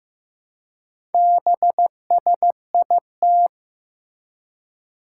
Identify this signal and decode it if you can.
Morse code, heard as a keyed tone, about 15 wpm, 700 Hz. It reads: BSIT
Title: Four Challenging Letter-Number Combinations 15wpm